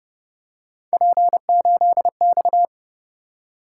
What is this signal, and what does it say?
Morse code, heard as a keyed tone, about 29 wpm, 700 Hz. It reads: P8X